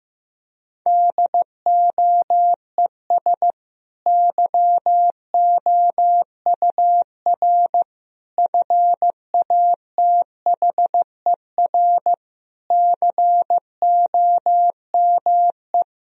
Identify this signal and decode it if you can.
Morse code, heard as a keyed tone, about 15 wpm, 700 Hz. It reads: DOES YOUR FATHER COME
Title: Sentences from Top 1000 Words - Review Part 2 - 15wpm